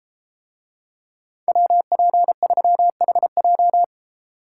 Morse code, heard as a keyed tone, about 32 wpm, 700 Hz. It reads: WP3HJ